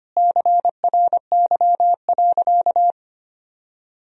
Morse code, heard as a keyed tone, about 25 wpm, 700 Hz. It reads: CRY.
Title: Sentences from Top 1000 Words - Review Part 2 - 25wpm